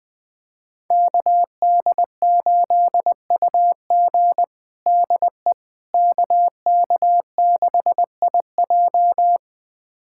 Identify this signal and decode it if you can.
Morse code, heard as a keyed tone, about 21 wpm, 700 Hz. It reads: KD8UG DE KK6IJ
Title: USA Callsign DE Callsign - 5 Characters - 20wpm